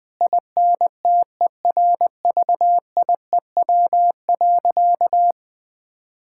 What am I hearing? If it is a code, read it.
Morse code, heard as a keyed tone, about 20 wpm, 700 Hz. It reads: INTERVIEW.
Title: Sentences from Top 1600 Words 20wpm